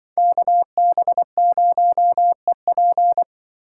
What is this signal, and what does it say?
Morse code, heard as a keyed tone, about 24 wpm, 700 Hz. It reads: KB0EP
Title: Speed Racing - USA 2x2 Call Signs 20wpm